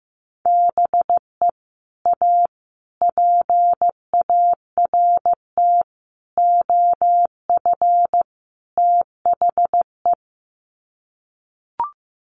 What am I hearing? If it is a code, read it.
Morse code, heard as a keyed tone, about 15 wpm, 700 Hz. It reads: BE A PART OF THE
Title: Sets of 5 Words Encore 15wpm